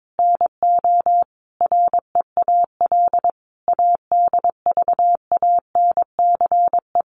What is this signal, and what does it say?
Morse code, heard as a keyed tone, about 22 wpm, 700 Hz. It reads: NO REAL ADVANCE
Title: Sentences from Top 1400 Words 22wpm